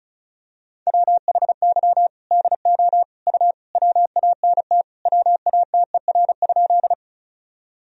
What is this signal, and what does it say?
Morse code, heard as a keyed tone, about 36 wpm, 700 Hz. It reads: WHY DO U WANT WATER?
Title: Sentences from Top 200 Words and Abbreviations 35wpm